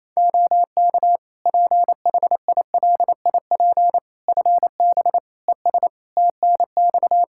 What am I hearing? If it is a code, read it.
Morse code, heard as a keyed tone, about 28 wpm, 700 Hz. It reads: OK PHILIP FB ES TNX
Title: K1USN SST QSO Practice 28wpm